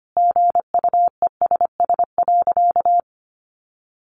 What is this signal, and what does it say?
Morse code, heard as a keyed tone, about 25 wpm, 700 Hz. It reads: GUESS.
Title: Sentences from Top 1500 Words 25wpm